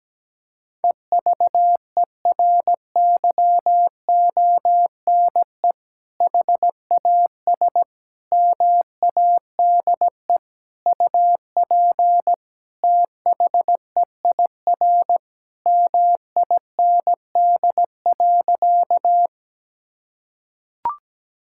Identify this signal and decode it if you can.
Morse code, heard as a keyed tone, about 17 wpm, 700 Hz. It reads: EVERYONE HAS MADE UP THEIR MIND.
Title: Sentences from Top 1000 Words - Review Part 5 - 17wpm